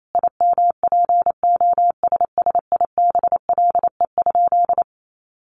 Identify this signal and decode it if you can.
Morse code, heard as a keyed tone, about 28 wpm, 700 Hz. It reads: IMPOSSIBLE?
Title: Sentences from Top 1100 Words 28wpm